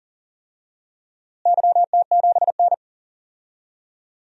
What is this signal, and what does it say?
Morse code, heard as a keyed tone, about 40 wpm, 700 Hz. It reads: YT7N
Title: DXCC-100 Call Signs - 4 Characters - Repeated 3x - 40wpm